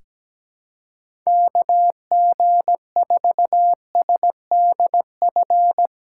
Morse code, heard as a keyed tone, about 17 wpm, 700 Hz. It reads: KG4SDF
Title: USA Callsign DE Callsign - 6 Characters - 17wpm